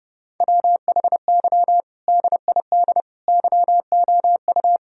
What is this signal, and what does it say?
Morse code, heard as a keyed tone, about 30 wpm, 700 Hz. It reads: WHY DID YOU
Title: Sentences from Top 1000 Words - Review Part 4 - 30wpm